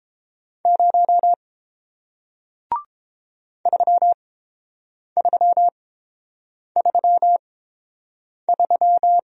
Morse code, heard as a keyed tone, about 30 wpm, spaced as for 7 wpm, 690 Hz. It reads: 0 3333